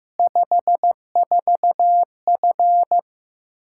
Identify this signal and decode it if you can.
Morse code, heard as a keyed tone, about 15 wpm, 700 Hz. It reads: 54F
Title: Practice QSO - Exchange 2 - Info - 15wpm